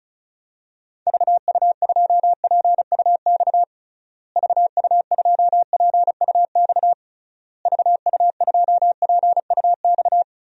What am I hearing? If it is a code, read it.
Morse code, heard as a keyed tone, about 36 wpm, 700 Hz. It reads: VU2PUX VU2PUX VU2PUX
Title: DXCC-100 Call Signs - 6 Characters - Repeated 3x - 35wpm